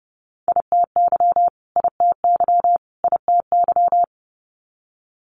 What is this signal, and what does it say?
Morse code, heard as a keyed tone, about 30 wpm, 700 Hz. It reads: ITY ITY ITY
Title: Multi-letter Phonemes Repeated 3x 30wpm